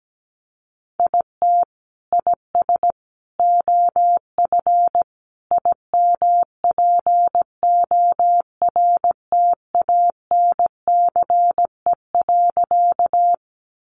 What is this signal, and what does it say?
Morse code, heard as a keyed tone, about 17 wpm, 700 Hz. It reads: IT IS OF IMPORTANCE.
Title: Sentences from Top 1200 Words 17wpm